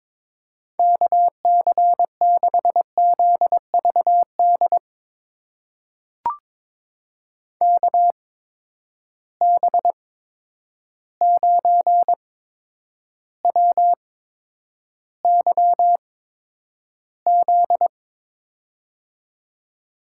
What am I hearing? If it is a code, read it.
Morse code, heard as a keyed tone, about 22 wpm, spaced as for 9 wpm, 700 Hz. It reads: KC6ZVD KB9WYZ
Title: USA Call Signs - 6 Characters - 8x Character Spacing - 22wpm